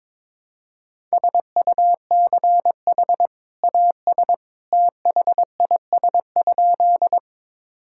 Morse code, heard as a keyed tone, about 21 wpm, 700 Hz. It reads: SUCH AS THIS?
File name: practice / 22wpm / Sentences with 2-Word Phrases - 22wpm